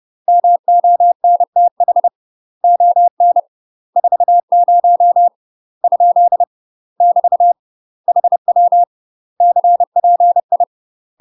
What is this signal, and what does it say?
Morse code, heard as a keyed tone, about 30 wpm, 700 Hz. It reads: MONTH ON 40 ? <BT> HW CPI
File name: UK MASTER QSOs TRAINING/30WPM